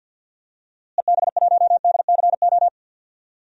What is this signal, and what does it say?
Morse code, heard as a keyed tone, about 52 wpm, 700 Hz. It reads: EB1DCO
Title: DXCC-100 Call Signs - 6 Characters - Repeated 3x - 50wpm